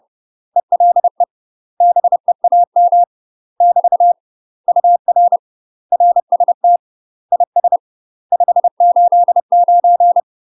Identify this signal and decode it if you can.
Morse code, heard as a keyed tone, about 30 wpm, 700 Hz. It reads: ELE BEAM <BT> UR RST IS 589
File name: UK MASTER QSOs TRAINING / 30WPM / UK MASTER QSOs 0024